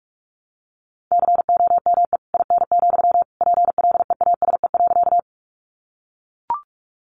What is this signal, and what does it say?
Morse code, heard as a keyed tone, about 44 wpm, 700 Hz. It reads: COME IN, PLEASE.
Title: Sentences from Top 1500 Words 45wpm